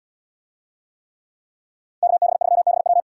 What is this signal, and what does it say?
Morse code, heard as a keyed tone, about 72 wpm, 700 Hz. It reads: KB3BW